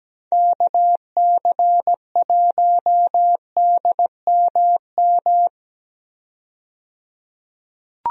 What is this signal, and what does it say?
Morse code, heard as a keyed tone, about 17 wpm, 700 Hz. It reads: KC1DMM
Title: USA Call Signs - 6 Characters - 17wpm